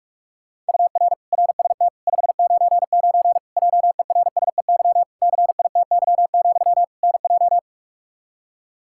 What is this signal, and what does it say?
Morse code, heard as a keyed tone, about 46 wpm, 700 Hz. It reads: UR RST 599 JERSEY CITY, NJ
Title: Practice QSO - Exchange 1 - Beginning - 45wpm